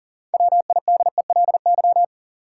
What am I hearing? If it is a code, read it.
Morse code, heard as a keyed tone, about 40 wpm, 700 Hz. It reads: WIDELY
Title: Sentences from Top 1200 Words - New Word Drill - 40wpm